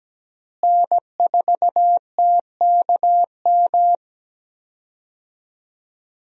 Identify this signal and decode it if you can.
Morse code, heard as a keyed tone, about 17 wpm, 700 Hz. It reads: N4TKM
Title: USA Call Signs - 5 Characters - Repeated 3x - 17wpm